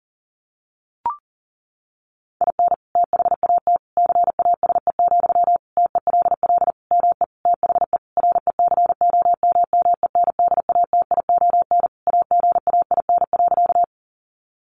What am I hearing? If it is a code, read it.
Morse code, heard as a keyed tone, about 40 wpm, 700 Hz. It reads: IN THAT CASE, TELL ME THE RECOMMENDATION AGAIN.
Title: Sentences from Top 1400 Words 40wpm